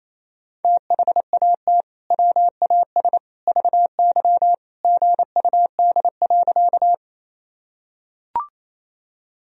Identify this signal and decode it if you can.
Morse code, heard as a keyed tone, about 28 wpm, 700 Hz. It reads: THAT WAS VY GUD.
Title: Sentences from Top 200 Words and Abbreviations 28wpm